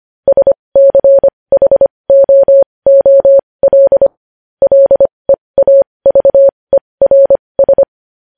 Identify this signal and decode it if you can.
Morse code, heard as a keyed tone, about 25 wpm, spaced as for 20 wpm, 550 Hz. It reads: SCHOOL LEAVERS
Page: news-headlines-in-morse-code-at-20-wpm